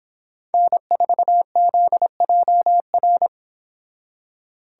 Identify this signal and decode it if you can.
Morse code, heard as a keyed tone, about 31 wpm, 700 Hz. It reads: N4ZJR